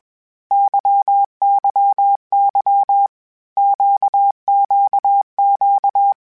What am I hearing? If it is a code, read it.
Morse code, heard as a keyed tone, about 21 wpm, 800 Hz. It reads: YYY QQQ
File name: Basic Sets / Basic Letter Groups Collective7 / LFCYQ@20